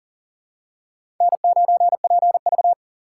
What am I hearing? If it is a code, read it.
Morse code, heard as a keyed tone, about 38 wpm, 700 Hz. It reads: N9PV
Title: USA Call Signs - 4 Characters - 6x Character Spacing - 40wpm